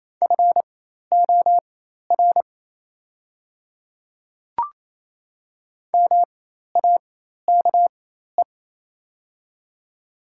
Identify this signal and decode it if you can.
Morse code, heard as a keyed tone, about 27 wpm, spaced as for 13 wpm, 700 Hz. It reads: FOR MAKE
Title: 4x Character Spacing - Top 100 Words - 28wpm